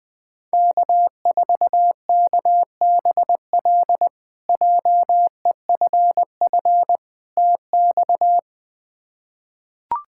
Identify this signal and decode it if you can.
Morse code, heard as a keyed tone, about 20 wpm, 700 Hz. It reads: K4KBL JEFF TX
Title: K1USN SST QSO Practice - 20wpm